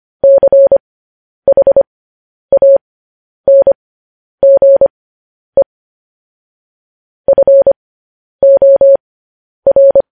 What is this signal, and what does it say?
Morse code, heard as a keyed tone, about 25 wpm, spaced as for 10 wpm, 550 Hz. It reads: CHANGE FOR
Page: news headlines in morse code at 10 wpm